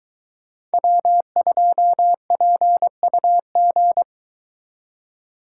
Morse code, heard as a keyed tone, about 23 wpm, 700 Hz. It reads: W2PUG